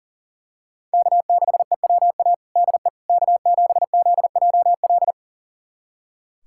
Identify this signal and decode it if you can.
Morse code, heard as a keyed tone, about 40 wpm, 700 Hz. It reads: K6EWA DE K7ZJL
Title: USA Callsign DE Callsign - 5 Characters - 40wpm